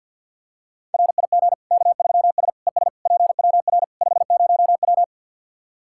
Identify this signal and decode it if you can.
Morse code, heard as a keyed tone, about 52 wpm, 700 Hz. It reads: RIG K3S ES PWR 50W
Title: K1USN SST QSO Practice 50wpm